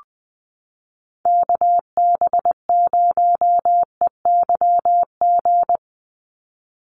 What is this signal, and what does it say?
Morse code, heard as a keyed tone, about 20 wpm, 700 Hz. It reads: KB0EYG
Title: USA Call Signs 20wpm